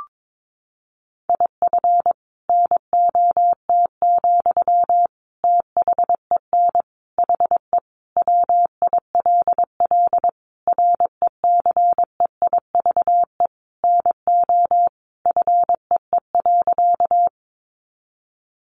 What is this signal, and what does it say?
Morse code, heard as a keyed tone, about 22 wpm, 700 Hz. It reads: IF NOT, THEN HE WILL RECEIVE NO FEE.
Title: Sentences from Top 1500 Words 22wpm